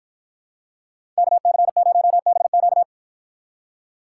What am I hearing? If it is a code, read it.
Morse code, heard as a keyed tone, about 52 wpm, 700 Hz. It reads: KC0BQ